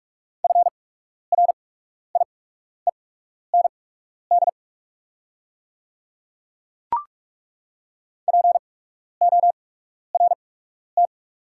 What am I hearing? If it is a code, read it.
Morse code, heard as a keyed tone, about 42 wpm, spaced as for 12 wpm, 700 Hz. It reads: FRIEND PORT
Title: 8x Character Spacing - Top 300 Words - 45wpm